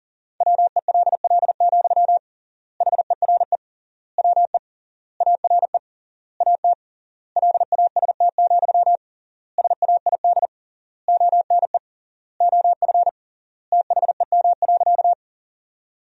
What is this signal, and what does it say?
Morse code, heard as a keyed tone, about 40 wpm, 700 Hz. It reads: WELL, HERE WE ARE AT LAST, SAID ONE OF THEM.
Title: 3x Word-Spacing - Sentences from Top 300 Words - 40wpm